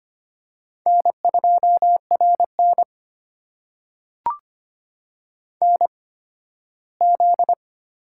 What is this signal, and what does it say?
Morse code, heard as a keyed tone, about 25 wpm, spaced as for 15 wpm, 700 Hz. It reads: N2RN NZ
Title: USA Call Signs - 4 Characters - 8x Character Spacing - 25wpm